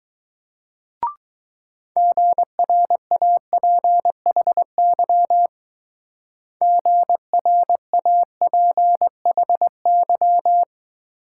Morse code, heard as a keyed tone, about 19 wpm, 700 Hz. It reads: GRAPHY GRAPHY